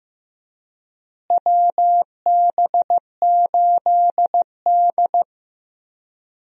Morse code, heard as a keyed tone, about 15 wpm, 700 Hz. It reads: WB8D
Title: USA Call Signs - 4 Characters - Repeated 3x - 15wpm